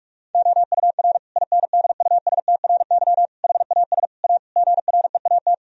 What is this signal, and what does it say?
Morse code, heard as a keyed tone, about 48 wpm, 700 Hz. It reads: OUR INDUSTRY HAS A GREAT
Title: Sentences from Top 1000 Words - Review Part 4 - 45wpm